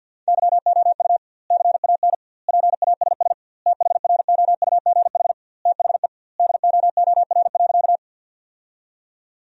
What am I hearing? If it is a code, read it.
Morse code, heard as a keyed tone, about 50 wpm, 700 Hz. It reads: YOU CAN PASS THROUGH THE DOOR.
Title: 2x Word-Spacing - Sentences from Top 400 Words - 50wpm